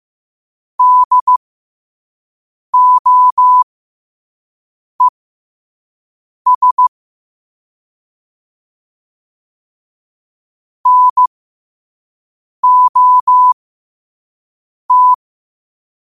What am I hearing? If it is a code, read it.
Morse code, heard as a keyed tone, about 15 wpm, spaced as for 5 wpm, 1000 Hz. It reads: DOES NOT
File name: qotdfiles / 5wpm